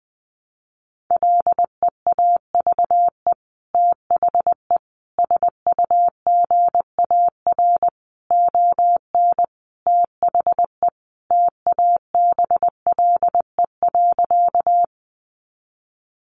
Morse code, heard as a keyed tone, about 20 wpm, 700 Hz. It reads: LEAVE THE SUGAR ON THE TABLE.
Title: Sentences from Top 1200 Words 20wpm